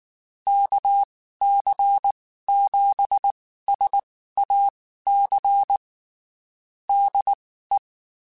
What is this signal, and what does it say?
Morse code, heard as a keyed tone, about 19 wpm, spaced as for 14 wpm, 800 Hz. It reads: KC7SAC DE
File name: (spaced) / QSOs Spaced-mp3s / kc7sac @18 sp